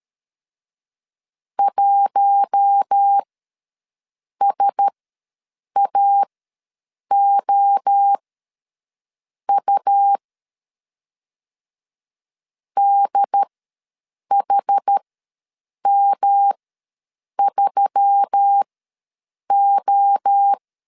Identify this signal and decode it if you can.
Morse code, heard as a keyed tone, about 13 wpm, spaced as for 7 wpm, 800 Hz. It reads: 1SAOU DHM3O
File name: LEZIONI / 5 LBVC / 5 LBVC ws 12wpm 3159